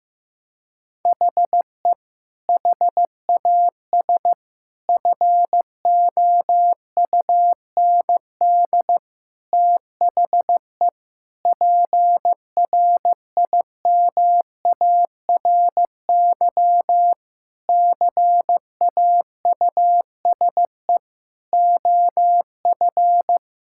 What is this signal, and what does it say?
Morse code, heard as a keyed tone, about 15 wpm, 700 Hz. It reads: HE HAS FOUND THE PRIMARY CAUSE OF
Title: Sentences from Top 1000 Words - Review Part 5 - 15wpm